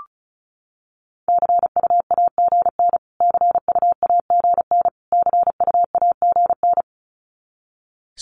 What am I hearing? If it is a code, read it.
Morse code, heard as a keyed tone, about 34 wpm, 700 Hz. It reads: CUAGN CUAGN CUAGN
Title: Abbreviations Repeated 3x - 35wpm